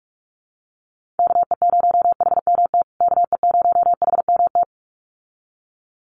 Morse code, heard as a keyed tone, about 44 wpm, 690 Hz. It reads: KE0HMT KE0HMT
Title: USA Call Signs Repeated 2x 45wpm